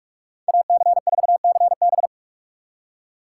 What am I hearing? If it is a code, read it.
Morse code, heard as a keyed tone, about 45 wpm, 700 Hz. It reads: AC4CB